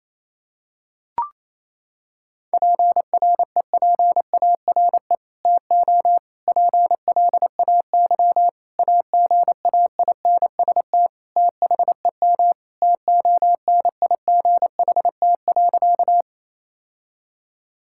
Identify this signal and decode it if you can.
Morse code, heard as a keyed tone, about 28 wpm, 700 Hz. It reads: PREPARE TO PLAY AGAINST THEM TONIGHT.
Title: Sentences from Top 1000 Words - Review Part 6 - 28wpm